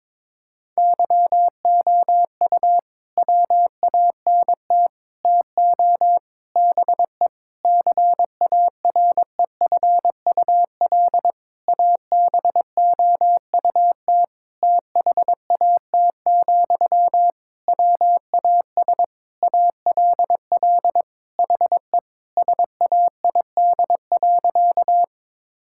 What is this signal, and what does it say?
Morse code, heard as a keyed tone, about 22 wpm, 700 Hz. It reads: YOU WANT TO BE CAREFUL ABOUT THAT, WAS ALL HE SAID.
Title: Sentences from Top 1600 Words 22wpm